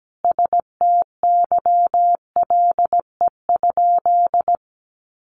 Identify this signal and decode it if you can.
Morse code, heard as a keyed tone, about 17 wpm, 700 Hz. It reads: STYLE?
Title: Sentences from Top 800 Words 17wpm